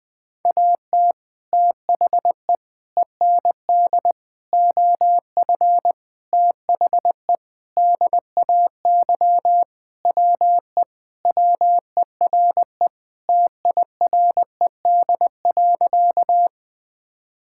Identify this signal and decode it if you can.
Morse code, heard as a keyed tone, about 20 wpm, 700 Hz. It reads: AT THE END OF THE DAY WE WERE TIRED.
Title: Sentences with 5-Word Phrases - 20wpm